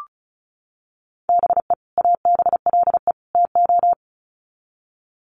Morse code, heard as a keyed tone, about 34 wpm, 700 Hz. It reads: BE ABLE TO
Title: Sets of 3 Words 35wpm